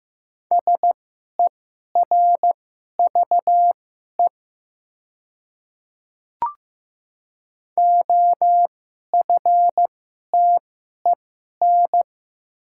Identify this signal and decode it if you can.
Morse code, heard as a keyed tone, about 15 wpm, spaced as for 11 wpm, 700 Hz. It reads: SERVE OFTEN
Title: 2x Character Spacing - Top 500 Words - 15wpm